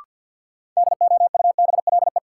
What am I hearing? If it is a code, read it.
Morse code, heard as a keyed tone, about 52 wpm, 700 Hz. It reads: DOUBLE
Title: Sentences from Top 1400 Words - New Word Drill - 50wpm